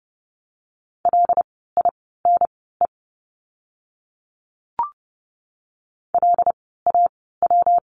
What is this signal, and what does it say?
Morse code, heard as a keyed tone, about 30 wpm, spaced as for 17 wpm, 700 Hz. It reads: LINE LAW
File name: course / 30:10wpm / 115 Lesson 19f - Letter w - Words 30wpm